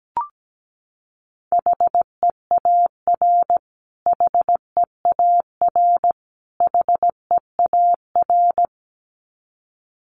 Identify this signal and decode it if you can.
Morse code, heard as a keyed tone, about 17 wpm, 700 Hz. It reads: HEAR HEAR HEAR
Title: Top 500 Words Repeated 17wpm